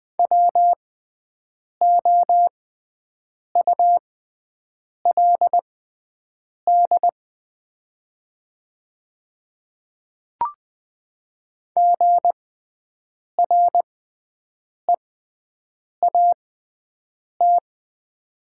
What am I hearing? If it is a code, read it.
Morse code, heard as a keyed tone, about 20 wpm, spaced as for 7 wpm, 700 Hz. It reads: WOULD GREAT
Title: 6x Character Spacing - Top 200 Words - 20wpm